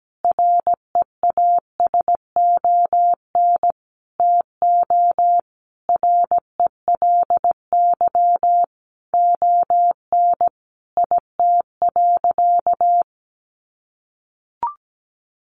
Morse code, heard as a keyed tone, about 17 wpm, 700 Hz. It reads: REASON TO RELY ON IT.
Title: Sentences from Top 1200 Words 17wpm